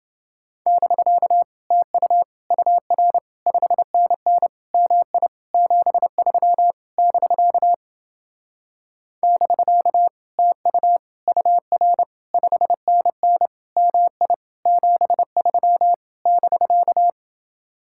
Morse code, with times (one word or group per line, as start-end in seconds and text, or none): 0.7-1.4 <BK>
1.7-2.2 TU
2.5-3.2 UR
3.5-4.5 5NN
4.7-5.3 MI
5.5-6.7 73
7.0-7.7 <BK>
9.2-10.1 <BK>
10.4-11.0 TU
11.3-12.0 UR
12.3-13.5 5NN
13.8-14.3 MI
14.7-15.9 73
16.3-17.1 <BK>